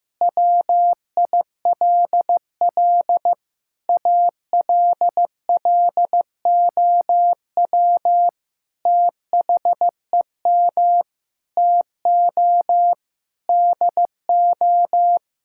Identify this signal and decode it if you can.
Morse code, heard as a keyed tone, about 15 wpm, 700 Hz. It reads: WILL ALLOW THEM TO DO
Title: Sentences from Top 1000 Words - Review Part 5 - 15wpm